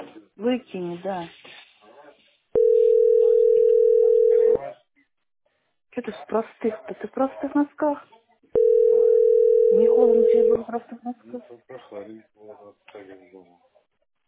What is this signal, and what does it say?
Ringback tone (North America), 440 + 480 Hz, with speech.
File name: f6cf4b37-d546-45c3-9eb4